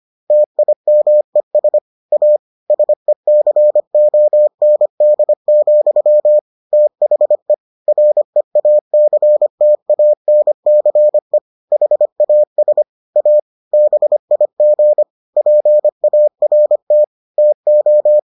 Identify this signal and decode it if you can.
Morse code, heard as a keyed tone, about 25 wpm, 600 Hz. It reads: TIMES A SECOND, THE REACTANCE HAS A BIG PART TO